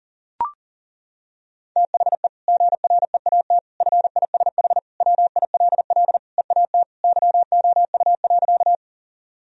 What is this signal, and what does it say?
Morse code, heard as a keyed tone, about 37 wpm, 700 Hz. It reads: THE GREAT FISH WILL EAT YOU.
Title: Sentences from Top 1000 Words - Review Part 1 - 40wpm